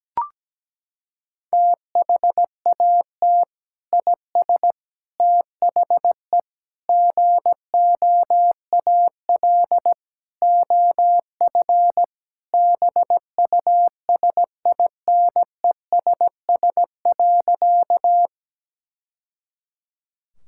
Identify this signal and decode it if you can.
Morse code, heard as a keyed tone, about 17 wpm, 700 Hz. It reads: THAT IS THE GOAL OF BUSINESS.
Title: Sentences from Top 1000 Words - Review Part 5 - 17wpm